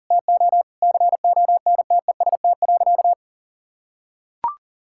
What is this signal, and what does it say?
Morse code, heard as a keyed tone, about 40 wpm, 700 Hz. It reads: TO CONTEST.
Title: Sentences from Top 1600 Words 40wpm